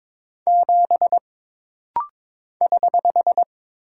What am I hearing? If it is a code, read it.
Morse code, heard as a keyed tone, about 22 wpm, 700 Hz. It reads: 7 <HH>
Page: This rapid-fire timing is geared to push your ICR (Instant Character Recognition) skills.